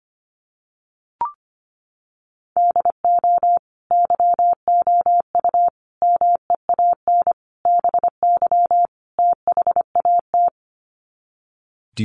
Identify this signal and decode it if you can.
Morse code, heard as a keyed tone, about 25 wpm, 700 Hz. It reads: DO YOU MEAN BY THAT
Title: Sets of 5 Words Encore2 25wpm